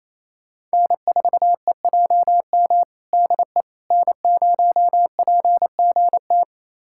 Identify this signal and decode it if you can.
Morse code, heard as a keyed tone, about 27 wpm, 700 Hz. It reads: N4EJM DE N0PGT
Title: USA Callsign DE Callsign - 5 Characters - 28wpm